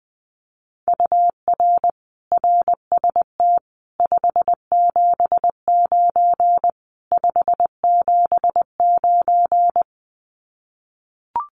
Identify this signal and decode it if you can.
Morse code, heard as a keyed tone, about 20 wpm, 700 Hz. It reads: UR RST 579 579
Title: Common QSO Phrases Encore 20wpm